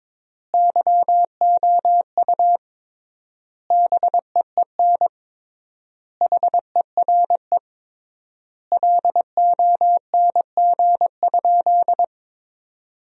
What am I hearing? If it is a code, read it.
Morse code, heard as a keyed tone, about 22 wpm, 700 Hz. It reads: YOU BEEN HERE LONG?
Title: 3x Word-Spacing - Sentences from Top 200 Words - 22wpm